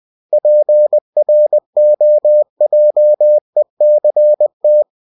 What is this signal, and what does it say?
Morse code, heard as a keyed tone, about 20 wpm, 600 Hz. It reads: PROJECT
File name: cw-books / winnie-the-pooh / winnie-20wpm-page-0113